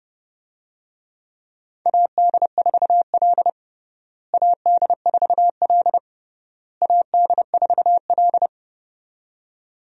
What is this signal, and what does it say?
Morse code, heard as a keyed tone, about 29 wpm, 700 Hz. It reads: AD4L AD4L AD4L
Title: USA Call Signs - 4 Characters - Repeated 3x - 30wpm